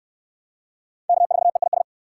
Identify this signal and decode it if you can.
Morse code, heard as a keyed tone, about 61 wpm, 700 Hz. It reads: K4EIS